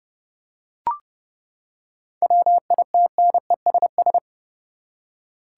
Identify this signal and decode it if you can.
Morse code, heard as a keyed tone, about 30 wpm, 700 Hz. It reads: WITNESS
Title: Seven Letter Words - 30wpm